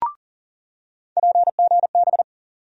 Morse code, heard as a keyed tone, about 40 wpm, 700 Hz. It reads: PGB
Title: USA Call Sign Postfixes - 3 Letters 40wpm